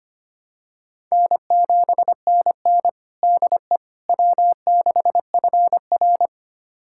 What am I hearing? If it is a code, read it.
Morse code, heard as a keyed tone, about 25 wpm, 700 Hz. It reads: N7NN DE W6FR
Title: USA Callsign DE Callsign - 4 Characters - 25wpm